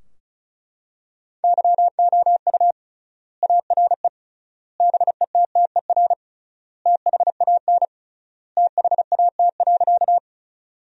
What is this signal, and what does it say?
Morse code, heard as a keyed tone, about 34 wpm, 700 Hz. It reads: YOU ARE BETTER THAN THAT.
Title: 3x Word-Spacing - Sentences from Top 400 Words - 35wpm